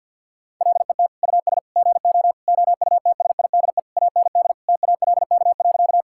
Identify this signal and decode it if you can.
Morse code, heard as a keyed tone, about 50 wpm, 700 Hz. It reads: LET US GO OUTSIDE AND TALK.
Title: Sentences from Top 1000 Words - Review Part 6 - 50wpm